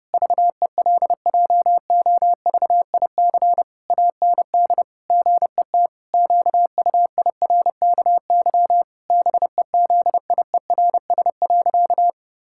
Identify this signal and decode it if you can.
Morse code, heard as a keyed tone, about 30 wpm, 700 Hz. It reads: VELJOVIC AND GET QUIRKY BEZIERS.
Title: Pangram Challenge v2 - 30wpm